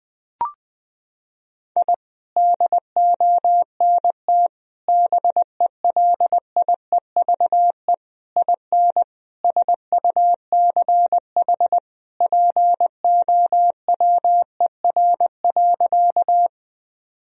Morse code, heard as a keyed tone, about 19 wpm, 700 Hz. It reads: I DONT BELIEVE IN SUCH POWER.
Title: Sentences from Top 1000 Words - Review Part 2 - 20wpm